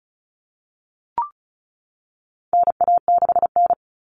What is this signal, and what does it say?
Morse code, heard as a keyed tone, about 34 wpm, 700 Hz. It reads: NA6N
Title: Common Cut Numbers - 4-Digit 35wpm